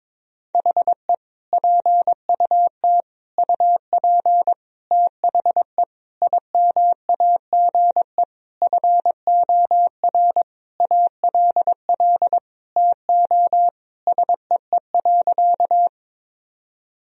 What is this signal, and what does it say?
Morse code, heard as a keyed tone, about 22 wpm, 700 Hz. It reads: HE PUT UP THE IMAGE FOR ALL TO SEE.
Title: Sentences from Top 1000 Words - Review Part 4 - 22wpm